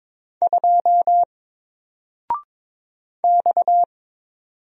Morse code, heard as a keyed tone, about 22 wpm, 700 Hz. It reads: 2 X
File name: Single-Letter-Number-22wpm